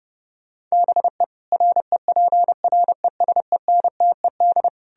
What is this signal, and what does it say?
Morse code, heard as a keyed tone, about 40 wpm, spaced as for 35 wpm, 700 Hz. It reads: BE REPRESENTED
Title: Speed-Racing - Sets of 5 Words 30wpm